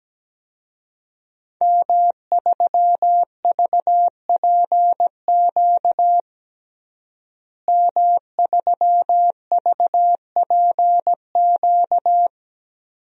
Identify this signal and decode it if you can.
Morse code, heard as a keyed tone, about 17 wpm, 700 Hz. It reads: M3VPQ M3VPQ
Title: DXCC-100 Call Signs - 5 Characters - Repeated 3x - 17wpm